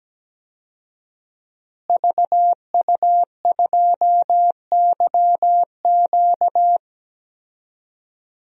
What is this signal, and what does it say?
Morse code, heard as a keyed tone, about 17 wpm, 700 Hz. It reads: VU2YQ
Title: DXCC-100 Call Signs - 5 Characters - Repeated 3x - 17wpm